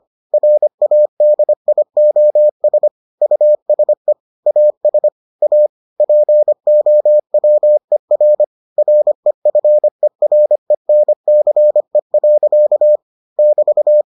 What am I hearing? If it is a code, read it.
Morse code, heard as a keyed tone, about 25 wpm, 600 Hz. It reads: RADIOS USE AS A POWER REFERENCE. <BT>